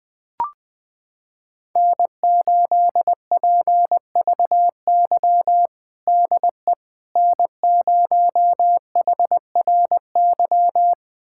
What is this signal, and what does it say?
Morse code, heard as a keyed tone, about 20 wpm, 700 Hz. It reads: N8PVY DE N0HRY
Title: USA Callsign DE Callsign - 5 Characters - 20wpm